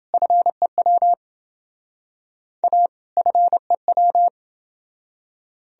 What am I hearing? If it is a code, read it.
Morse code, heard as a keyed tone, about 28 wpm, 700 Hz. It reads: FEW A FEW